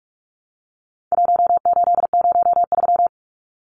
Morse code, heard as a keyed tone, about 44 wpm, 700 Hz. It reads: 1803
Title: Numbers - 4-Digit 45wpm